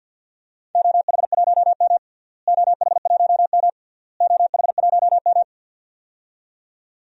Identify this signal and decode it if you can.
Morse code, heard as a keyed tone, about 47 wpm, 700 Hz. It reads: OH1M OH1M OH1M
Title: DXCC-100 Call Signs - 4 Characters - Repeated 3x - 50wpm